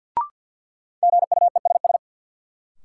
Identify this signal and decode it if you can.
Morse code, heard as a keyed tone, about 52 wpm, 700 Hz. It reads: GUESS
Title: Sentences from Top 1500 Words - New Word Drill - 50wpm